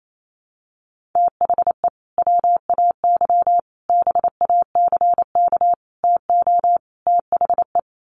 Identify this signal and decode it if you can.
Morse code, heard as a keyed tone, about 28 wpm, 700 Hz. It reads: THE WAY BACK TO THE
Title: Sets of 5 Words Encore2 28wpm